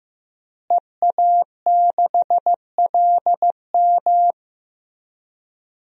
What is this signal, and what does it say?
Morse code, heard as a keyed tone, about 15 wpm, 700 Hz. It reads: EA6LM
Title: DXCC-100 Call Signs - 5 Characters - Repeated 3x - 15wpm